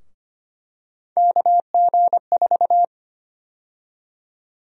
Morse code, heard as a keyed tone, about 25 wpm, 700 Hz. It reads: KG4
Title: USA Call Sign Prefixes - 25wpm